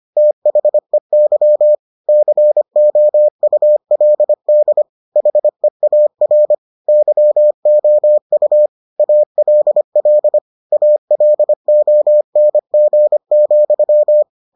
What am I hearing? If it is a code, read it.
Morse code, heard as a keyed tone, about 25 wpm, 600 Hz. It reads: THEY COULD HEAR YOU ALL ALONG,